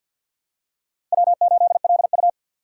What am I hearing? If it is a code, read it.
Morse code, heard as a keyed tone, about 51 wpm, 700 Hz. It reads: W8LU
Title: USA Call Signs - 4 Characters - 2x Character Spacing - 50wpm